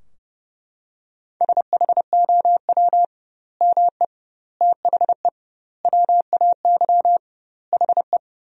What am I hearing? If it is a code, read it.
Morse code, heard as a keyed tone, about 30 wpm, 700 Hz. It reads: SHOW ME THE WAY HE
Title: 2x Word-Spacing - Sentences from Top 200 Words - 30wpm